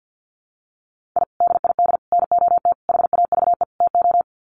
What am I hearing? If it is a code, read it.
Morse code, heard as a keyed tone, about 50 wpm, 700 Hz. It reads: I DID NOT HAVE TO
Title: Sets of 5 Words 50wpm